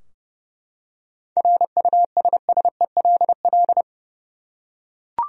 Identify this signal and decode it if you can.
Morse code, heard as a keyed tone, about 32 wpm, 700 Hz. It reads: RUSSELL
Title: USA Common Names - 30wpm